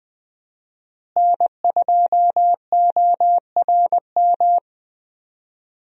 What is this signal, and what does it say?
Morse code, heard as a keyed tone, about 21 wpm, 700 Hz. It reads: N2ORM
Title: USA Call Signs - 5 Characters - Repeated 3x - 20wpm